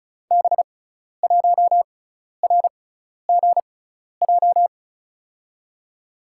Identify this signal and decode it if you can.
Morse code, heard as a keyed tone, about 36 wpm, spaced as for 12 wpm, 700 Hz. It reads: B1RGJ
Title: USA Call Signs - 6 Characters - 6x Character Spacing - 35wpm